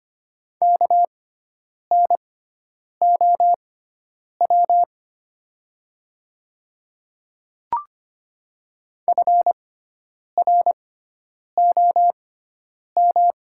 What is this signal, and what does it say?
Morse code, heard as a keyed tone, about 25 wpm, spaced as for 9 wpm, 700 Hz. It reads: KNOW FROM
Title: 6x Character Spacing - Top 100 Words - 25wpm